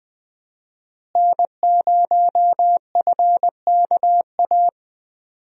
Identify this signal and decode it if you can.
Morse code, heard as a keyed tone, about 20 wpm, 700 Hz. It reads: N0FKA
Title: USA Call Signs - 5 Characters - Repeated 3x - 20wpm